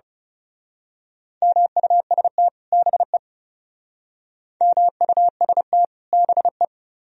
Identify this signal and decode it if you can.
Morse code, heard as a keyed tone, about 34 wpm, 700 Hz. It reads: MUST BE MUST BE
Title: Speed-Racing - Sets of 2 Words 30wpm